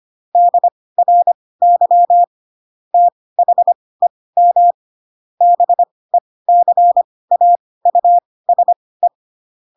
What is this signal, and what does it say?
Morse code, heard as a keyed tone, about 25 wpm, spaced as for 18 wpm, 700 Hz. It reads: DRY THEM BECAUSE
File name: Pinocchio / Pinocchio ch7 12 25 ik1hge